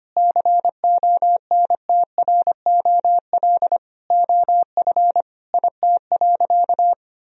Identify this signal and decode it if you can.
Morse code, heard as a keyed tone, about 25 wpm, 700 Hz. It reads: CONTROL OF IT.
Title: Sentences from Top 1000 Words - Review Part 3 - 25wpm